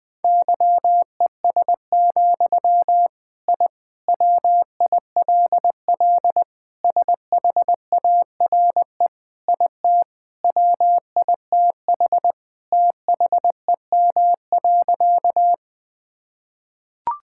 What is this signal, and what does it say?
Morse code, heard as a keyed tone, about 21 wpm, 700 Hz. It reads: YES, I WILL SHARE IT WITH THEM.
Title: Sentences from Top 1000 Words - Review Part 6 - 20wpm